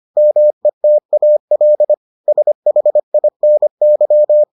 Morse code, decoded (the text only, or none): METAL SHINY